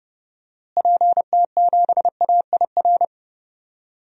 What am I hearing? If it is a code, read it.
Morse code, heard as a keyed tone, about 30 wpm, 700 Hz. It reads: PT7AIR
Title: DXCC-100 Call Signs - 6 Characters - Repeated 2x - 30wpm